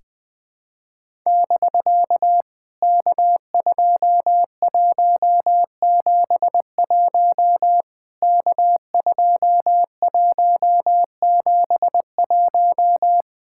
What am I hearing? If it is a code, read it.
Morse code, heard as a keyed tone, about 19 wpm, 700 Hz. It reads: <BK> K2171 K2171
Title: POTA QSO Practice - 20wpm